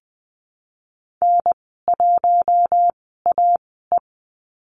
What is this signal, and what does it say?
Morse code, heard as a keyed tone, about 20 wpm, spaced as for 14 wpm, 700 Hz. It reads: N1AE
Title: 042 Lesson 8g - Number 1 - Call Signs 20wpm